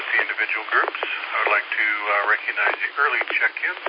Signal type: Emergency Alert System (SAME) tones